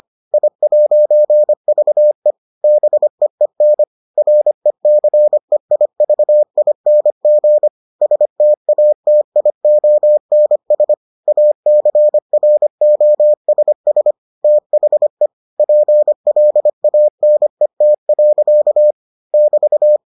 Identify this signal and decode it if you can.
Morse code, heard as a keyed tone, about 25 wpm, 600 Hz. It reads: I'VE BEEN RECEIVING STATIONS ACROSS THE PLANET. <BT>